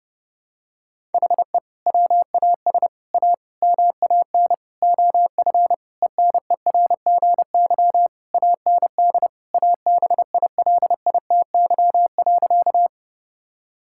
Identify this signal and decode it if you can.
Morse code, heard as a keyed tone, about 30 wpm, 700 Hz. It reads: HE WAS A MAN OF ENERGY AND ABILITY.
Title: Sentences from Top 1000 Words - Review Part 6 - 30wpm